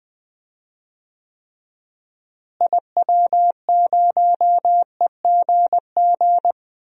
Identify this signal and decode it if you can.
Morse code, heard as a keyed tone, about 20 wpm, 700 Hz. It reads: IW0EGG
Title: DXCC-100 Call Signs - 6 Characters - Repeated 2x - 20wpm